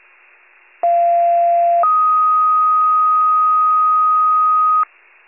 Two-tone page, heard as a one-second tone, then a three-second tone, about 690 then 1250 Hz. It is heard from 0.8 to 4.8 s.
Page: 21 FIRE